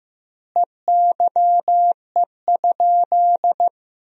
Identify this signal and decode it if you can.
Morse code, heard as a keyed tone, about 15 wpm, 700 Hz. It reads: EYE?
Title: 2x Word-Spacing - Sentences from Top 300 Words - 15wpm